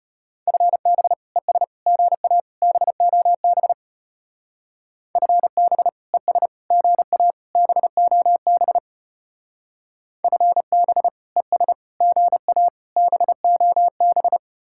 Morse code, heard as a keyed tone, about 33 wpm, 700 Hz. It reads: FB ES GA BOB FB ES GA BOB FB ES GA BOB